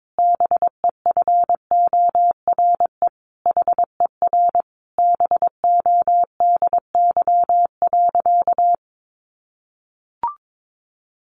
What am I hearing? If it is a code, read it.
Morse code, heard as a keyed tone, about 22 wpm, 700 Hz. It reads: BEFORE HER BODY.
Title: Sentences from Top 1500 Words 22wpm